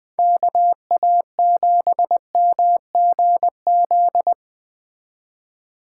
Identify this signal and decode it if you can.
Morse code, heard as a keyed tone, about 20 wpm, 700 Hz. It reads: KA7MGZ